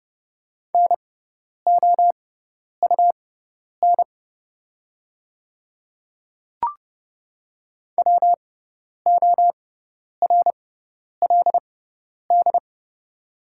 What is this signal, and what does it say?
Morse code, heard as a keyed tone, about 30 wpm, spaced as for 10 wpm, 700 Hz. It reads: NOUN WORLD
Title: 6x Character Spacing - Top 500 Words - 30wpm